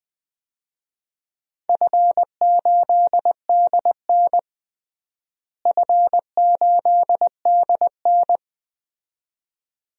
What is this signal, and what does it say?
Morse code, heard as a keyed tone, about 20 wpm, 700 Hz. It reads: F8DN F8DN
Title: DXCC-100 Call Signs - 4 Characters - Repeated 2x - 20wpm